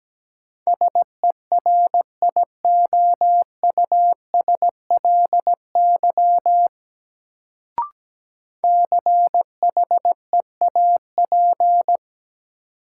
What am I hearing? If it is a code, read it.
Morse code, heard as a keyed tone, about 17 wpm, 700 Hz. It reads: SERIOUSLY CHEAP
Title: Sentences from Top 1200 Words - New Word Drill Alt - 17wpm